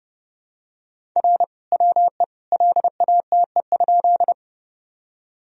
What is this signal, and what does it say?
Morse code, heard as a keyed tone, about 31 wpm, 700 Hz. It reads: R WE LATE?
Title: Sentences from Top 300 Words and Abbreviations 30wpm